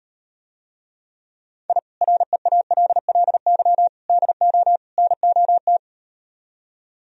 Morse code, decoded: I REALLY DO NOT

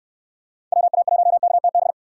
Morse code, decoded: WA1KTB

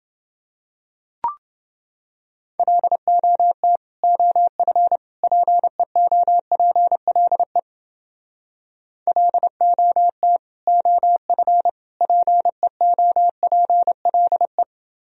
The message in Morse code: LOT OF PEOPLE LOT OF PEOPLE

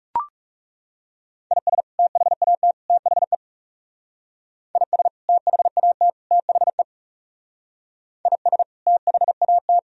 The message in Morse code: IS THAT THE IS THAT THE IS THAT

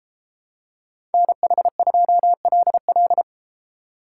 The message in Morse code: NH2LL

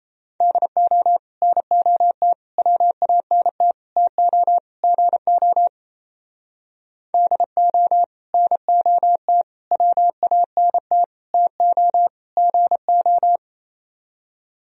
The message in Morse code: DO NOT WANT TO GO DO NOT WANT TO GO

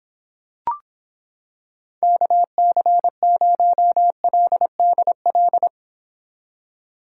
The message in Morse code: KC0LDL

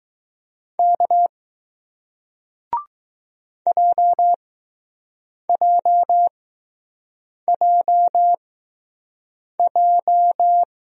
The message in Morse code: K JJJJ

